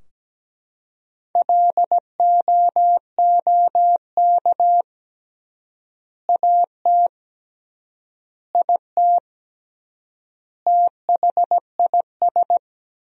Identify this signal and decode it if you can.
Morse code, heard as a keyed tone, about 17 wpm, 700 Hz. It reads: LOOK AT IT THIS
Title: 3x Word-Spacing - Sentences from Top 100 Words - 17wpm